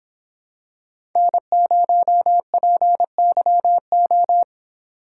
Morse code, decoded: N0PYO

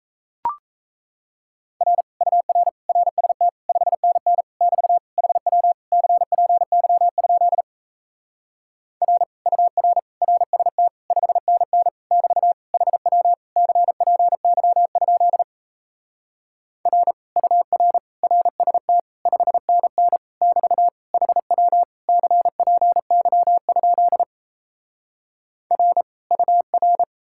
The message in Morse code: R UR RST 5NN <BT> HW CPY? R UR RST 5NN <BT> HW CPY? R UR RST 5NN <BT> HW CPY? R UR

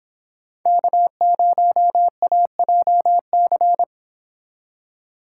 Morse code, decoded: K0AJC

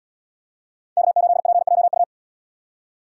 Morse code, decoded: K8PJU

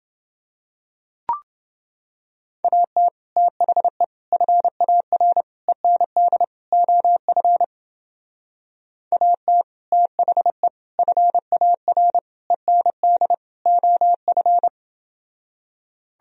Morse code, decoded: AT THE FAR END OF AT THE FAR END OF

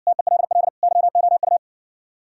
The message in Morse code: TELL YOU

